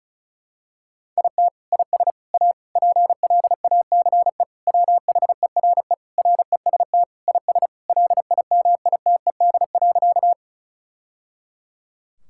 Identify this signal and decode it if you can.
Morse code, heard as a keyed tone, about 36 wpm, 700 Hz. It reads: IT IS A PLACE WHERE REST IS LIMITED.
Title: Sentences from Top 1000 Words - Review Part 6 - 35wpm